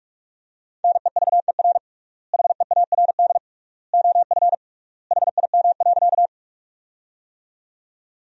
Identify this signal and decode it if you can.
Morse code, heard as a keyed tone, about 42 wpm, 700 Hz. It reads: NEVER HEARD OF HIM.
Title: 3x Word-Spacing - Sentences from Top 400 Words - 45wpm